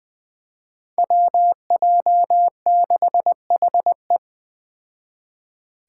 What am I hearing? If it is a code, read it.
Morse code, heard as a keyed tone, about 20 wpm, 700 Hz. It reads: WJ6HE
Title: Five Challenging Letter-Number Combinations 20wpm